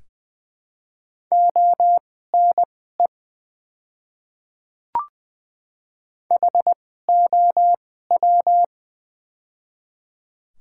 Morse code, heard as a keyed tone, about 20 wpm, spaced as for 15 wpm, 700 Hz. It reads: ONE HOW
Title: 2x Character Spacing - Top 200 Words - 20wpm